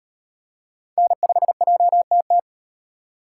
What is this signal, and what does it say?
Morse code, heard as a keyed tone, about 38 wpm, 700 Hz. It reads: N5JTT